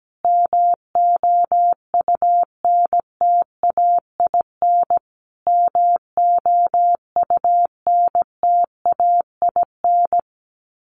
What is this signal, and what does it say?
Morse code, heard as a keyed tone, about 17 wpm, 700 Hz. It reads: MOUNTAIN MOUNTAIN
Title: Top 300 Words Repeated 17wpm